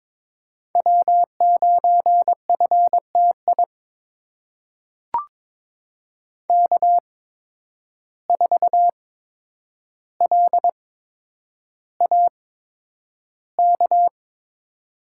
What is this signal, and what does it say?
Morse code, heard as a keyed tone, about 22 wpm, spaced as for 9 wpm, 700 Hz. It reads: W9FTI K4LAK
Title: USA Call Signs - 5 Characters - 8x Character Spacing - 22wpm